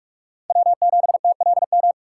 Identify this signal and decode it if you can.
Morse code, heard as a keyed tone, about 45 wpm, 700 Hz. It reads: W7TLM
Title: USA Call Signs - 5 Characters - 3x Character Spacing - 45wpm